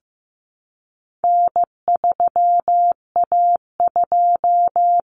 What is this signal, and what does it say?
Morse code, heard as a keyed tone, about 15 wpm, 700 Hz. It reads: N3A2
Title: Common Cut Numbers - 4-Digit 15wpm